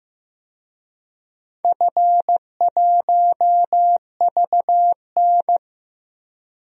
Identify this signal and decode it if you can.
Morse code, heard as a keyed tone, about 15 wpm, 700 Hz. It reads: F1VN
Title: DXCC-100 Call Signs - 4 Characters - Repeated 3x - 15wpm